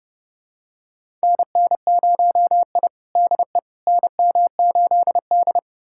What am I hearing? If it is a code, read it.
Morse code, heard as a keyed tone, about 30 wpm, 700 Hz. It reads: NN0I DE NM8D
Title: USA Callsign DE Callsign - 4 Characters - 30wpm